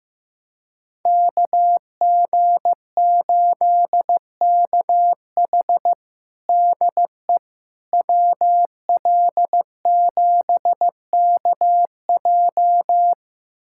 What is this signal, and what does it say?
Morse code, heard as a keyed tone, about 15 wpm, 700 Hz. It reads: KG8KH DE WL7KJ
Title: USA Callsign DE Callsign - 5 Characters - 15wpm